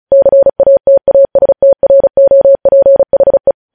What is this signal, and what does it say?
Morse code, heard as a keyed tone, about 36 wpm, 550 Hz. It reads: CATASTROPHE
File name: podcast / News headlines-1x-35x35